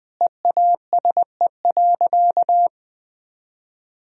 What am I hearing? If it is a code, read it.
Morse code, heard as a keyed tone, about 21 wpm, 700 Hz. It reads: EASE.
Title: Sentences from Top 1000 Words - Review Part 1 - 20wpm